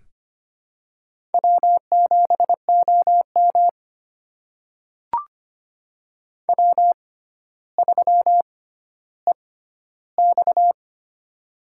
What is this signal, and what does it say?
Morse code, heard as a keyed tone, about 25 wpm, spaced as for 13 wpm, 700 Hz. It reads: W7OM W3EX
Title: USA Call Signs - 4 Characters - 6x Character Spacing - 25wpm